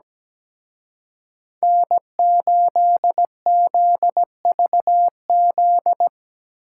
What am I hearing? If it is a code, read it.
Morse code, heard as a keyed tone, about 20 wpm, 700 Hz. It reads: N8ZVZ
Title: Speed Racing - USA 1x3 Call Signs 17wpm